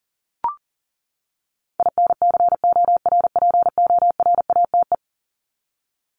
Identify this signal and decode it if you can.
Morse code, heard as a keyed tone, about 40 wpm, 700 Hz. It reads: INCORPORATE
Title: Top 1751-2000 Words 40wpm